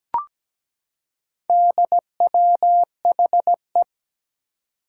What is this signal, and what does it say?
Morse code, heard as a keyed tone, about 17 wpm, 700 Hz. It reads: DWHE
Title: Four Challenging Letter-Number Combinations 17wpm